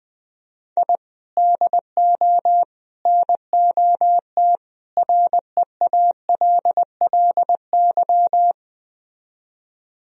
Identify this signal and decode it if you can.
Morse code, heard as a keyed tone, about 20 wpm, 700 Hz. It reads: I DO NOT REALLY
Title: Speed-Racing - Sets of 4 Words 17wpm